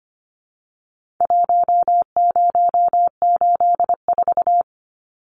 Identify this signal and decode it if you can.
Morse code, heard as a keyed tone, about 25 wpm, 700 Hz. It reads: 1084
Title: Numbers - Contest Style - 4-Digit 25wpm